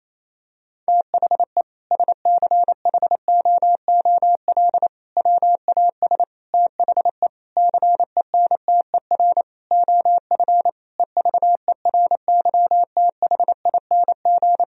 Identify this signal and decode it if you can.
Morse code, heard as a keyed tone, about 28 wpm, 700 Hz. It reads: THE SCHOOL WAS THE CENTER OF EVERYTHING
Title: Sentences from Top 1000 Words - Review Part 2 - 28wpm